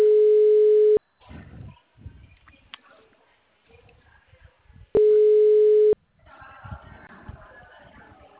Ringback tone (Europe, 425 Hz).
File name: out- -2002- - - .54